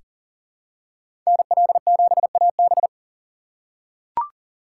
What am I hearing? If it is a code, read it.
Morse code, heard as a keyed tone, about 40 wpm, 700 Hz. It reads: NL7AB
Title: USA Call Signs - 3x Character Spacing - 40wpm